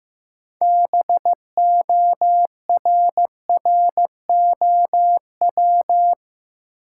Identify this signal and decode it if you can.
Morse code, heard as a keyed tone, about 15 wpm, 700 Hz. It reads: BORROW
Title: Six Letter Words - 15wpm